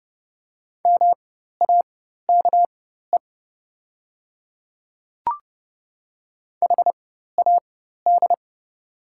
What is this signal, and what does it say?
Morse code, heard as a keyed tone, about 30 wpm, spaced as for 14 wpm, 700 Hz. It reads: MAKE HAD
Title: 4x Character Spacing - Top 500 Words - 30wpm